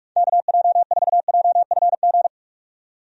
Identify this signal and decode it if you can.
Morse code, heard as a keyed tone, about 42 wpm, 700 Hz. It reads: KJ4JFG
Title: USA Call Signs - 6 Characters - 45wpm